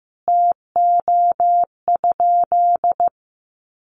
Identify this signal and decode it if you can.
Morse code, heard as a keyed tone, about 14 wpm, 700 Hz. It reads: TO?
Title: Sentences from Top 100 Words 15wpm